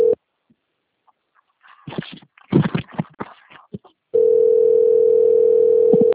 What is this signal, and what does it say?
Ringback tone (North America), 440 + 480 Hz.
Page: Itatiba II